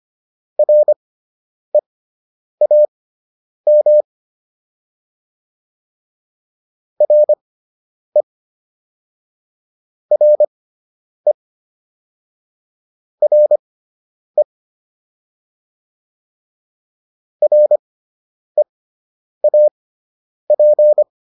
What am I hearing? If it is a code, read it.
Morse code, heard as a keyed tone, about 25 wpm, spaced as for 9 wpm, 600 Hz. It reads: REAM RE RE RE REAP